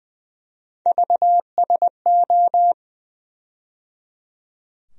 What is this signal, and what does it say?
Morse code, heard as a keyed tone, about 20 wpm, 700 Hz. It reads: VSO
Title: USA Call Sign Postfixes - 3 Letters 20wpm